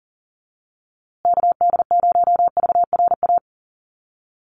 Morse code, heard as a keyed tone, about 40 wpm, 700 Hz. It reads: KD0VRA
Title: USA Call Signs Repeated 2x 40wpm